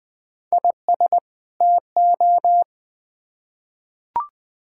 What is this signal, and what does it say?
Morse code, heard as a keyed tone, about 20 wpm, 700 Hz. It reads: IS TO
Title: Build A Phrase - 2-Word Phrases - 20wpm